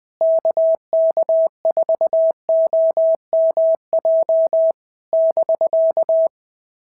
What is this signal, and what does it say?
Morse code, heard as a keyed tone, about 20 wpm, 650 Hz. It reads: KK4OMJ <BK>
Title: POTA QSO Practice - 20wpm